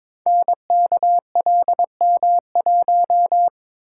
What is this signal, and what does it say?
Morse code, heard as a keyed tone, about 22 wpm, 700 Hz. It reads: NKLM1
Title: Code Groups - 5 Letter:Numbers - 22wpm